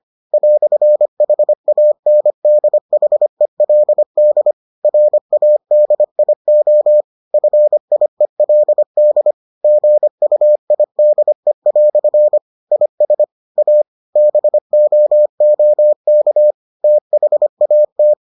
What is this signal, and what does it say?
Morse code, heard as a keyed tone, about 25 wpm, 600 Hz. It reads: "HANDHELD RADIO FIELD GUIDE" IS A BOOK THAT